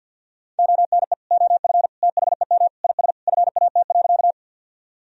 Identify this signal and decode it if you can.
Morse code, heard as a keyed tone, about 50 wpm, 700 Hz. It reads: ONE OF THEM IS FAT.
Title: Sentences from Top 1000 Words - Review Part 4 - 50wpm